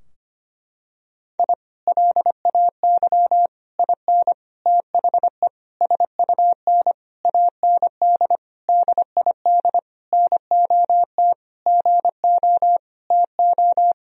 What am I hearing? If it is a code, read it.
Morse code, heard as a keyed tone, about 25 wpm, 700 Hz. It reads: I LAY IN THE SUN AND DID NOT GO TO
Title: Sentences from Top 1000 Words - Review Part 2 - 25wpm